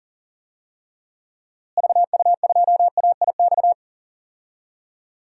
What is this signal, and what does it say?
Morse code, heard as a keyed tone, about 40 wpm, 700 Hz. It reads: VU2AIX
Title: DXCC-100 Call Signs - 6 Characters - Repeated 2x - 40wpm